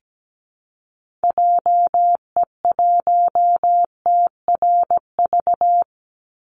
Morse code, heard as a keyed tone, about 17 wpm, 700 Hz. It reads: JE1TRV
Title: Call Signs Repeated 2x - 17wpm